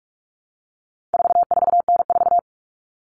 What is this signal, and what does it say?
Morse code, heard as a keyed tone, about 44 wpm, 700 Hz. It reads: 44N4
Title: All Cut Numbers - 4-Digit 45wpm